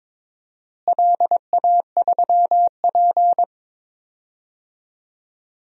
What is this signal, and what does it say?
Morse code, heard as a keyed tone, about 22 wpm, 700 Hz. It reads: LA3P
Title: DXCC-100 Call Signs - 4 Characters - Repeated 2x - 22wpm